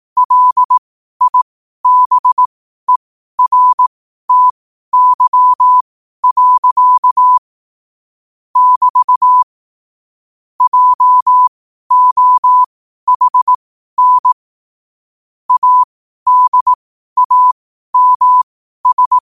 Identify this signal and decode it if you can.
Morse code, heard as a keyed tone, about 18 wpm, spaced as for 12 wpm, 1000 Hz. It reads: LIBERTY. <BT> JOHN ADAMS
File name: qotdfiles / 12wpm